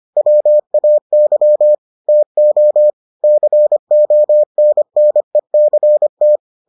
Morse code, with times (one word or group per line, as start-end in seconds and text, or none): 0.2-1.7 WAY
2.1-2.9 TO
3.2-6.4 CONNECT